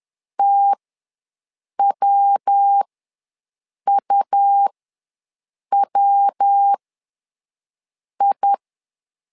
Morse code, heard as a keyed tone, about 10 wpm, spaced as for 6 wpm, 800 Hz. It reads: TWUWI
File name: LEZIONI / 3 MWRO / 3 MW ws 10wpm 2213